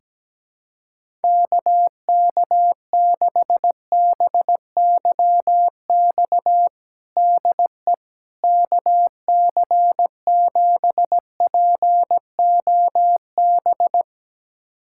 Morse code, heard as a keyed tone, about 17 wpm, 700 Hz. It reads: KK6BYX DE KC7POB
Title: USA Callsign DE Callsign - 6 Characters - 17wpm